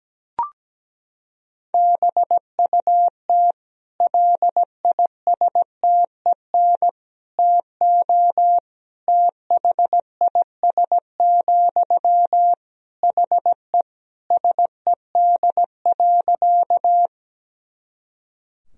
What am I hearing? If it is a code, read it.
Morse code, heard as a keyed tone, about 17 wpm, 700 Hz. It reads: BUT LISTEN TO THIS, HE SED.
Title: Sentences from Top 400 Words and Abbreviations 17wpm